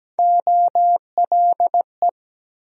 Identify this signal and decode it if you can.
Morse code, heard as a keyed tone, about 17 wpm, 700 Hz. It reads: OLE
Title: USA Call Sign Postfixes - 3 Letters 17wpm